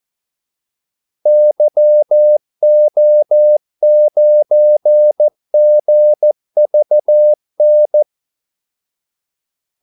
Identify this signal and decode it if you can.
Morse code, heard as a keyed tone, about 14 wpm, 600 Hz. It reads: YO9GVN
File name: Callsigns-14wpm